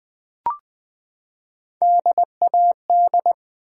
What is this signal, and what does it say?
Morse code, heard as a keyed tone, about 20 wpm, 700 Hz. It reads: DAD
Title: Three Letter Words - 20wpm